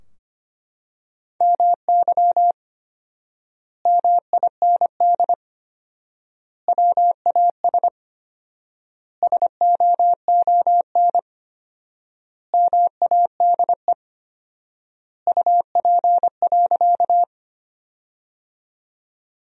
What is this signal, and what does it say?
Morse code, heard as a keyed tone, about 25 wpm, 700 Hz. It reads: MY MIND WAS SOON MADE UP.
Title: 4x Word-Spacing - Sentences from Top 500 Words - 25wpm